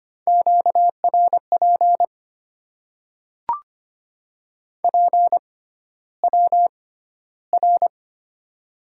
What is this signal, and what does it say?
Morse code, heard as a keyed tone, about 25 wpm, spaced as for 13 wpm, 700 Hz. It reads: QRP PWR
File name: Top 90 QSO Elements - 6x Character Spacing - 25wpm